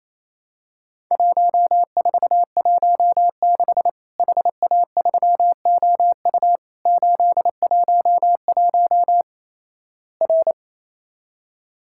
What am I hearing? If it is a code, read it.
Morse code, heard as a keyed tone, about 28 wpm, 700 Hz. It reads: 1416 HA3OU 811 R
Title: Worked All Europe DX Contest - QTC Exchange - 28wpm